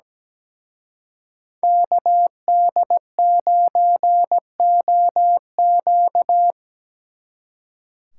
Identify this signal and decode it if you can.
Morse code, heard as a keyed tone, about 20 wpm, 700 Hz. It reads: KD9OQ
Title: Speed Racing - USA 2x2 Call Signs 17wpm